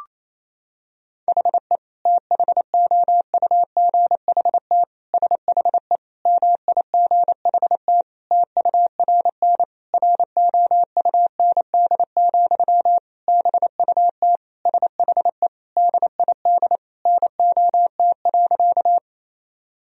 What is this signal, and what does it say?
Morse code, heard as a keyed tone, about 28 wpm, 700 Hz. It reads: HE THOUGHT SHE MIGHT TURN ROUND, BUT SHE DID NOT.
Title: Sentences from Top 1000 Words - Review Part 1 - 28wpm